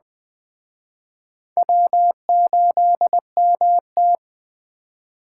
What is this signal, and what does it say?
Morse code, heard as a keyed tone, about 21 wpm, 700 Hz. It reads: W8MT
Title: Speed Racing - USA 1x2 Call Signs 17wpm